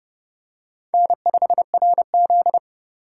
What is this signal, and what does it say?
Morse code, heard as a keyed tone, about 29 wpm, 700 Hz. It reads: N5RZ